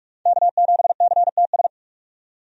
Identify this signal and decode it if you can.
Morse code, heard as a keyed tone, about 46 wpm, 700 Hz. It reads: K7CTS
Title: USA Call Signs - 5 Characters - 45wpm